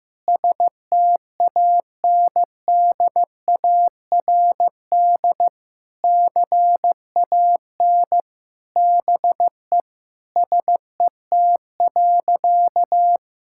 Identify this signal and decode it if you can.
Morse code, heard as a keyed tone, about 15 wpm, 700 Hz. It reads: STANDARD CAN BE SET.
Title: Sentences from Top 1000 Words - Review Part 6 - 15wpm